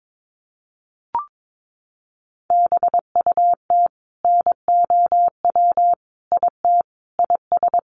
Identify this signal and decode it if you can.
Morse code, heard as a keyed tone, about 22 wpm, 700 Hz. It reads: BUT NOW IT IS
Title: Sentences from Top 600 Words 22wpm